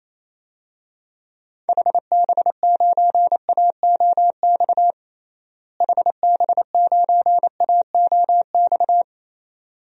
Morse code, heard as a keyed tone, about 27 wpm, 700 Hz. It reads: HB9AOX HB9AOX
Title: DXCC-100 Call Signs - 6 Characters - Repeated 2x - 28wpm